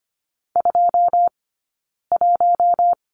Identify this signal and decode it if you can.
Morse code, heard as a keyed tone, about 25 wpm, spaced as for 9 wpm, 690 Hz. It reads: 21
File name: Lesson 15d - Number 5 - Two Characters 25wpm